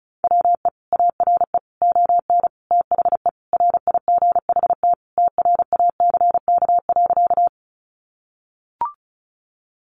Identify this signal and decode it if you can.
Morse code, heard as a keyed tone, about 34 wpm, 700 Hz. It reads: WE ARE ON THE RIGHT TRACK.
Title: Sentences from Top 1100 Words 35wpm